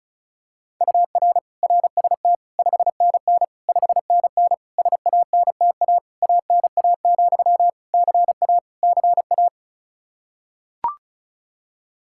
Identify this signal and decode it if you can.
Morse code, heard as a keyed tone, about 36 wpm, 700 Hz. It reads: UR RST 5NN 5NN SANTA ANA, CA CA
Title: Practice QSO - Exchange 1 - Beginning - 35wpm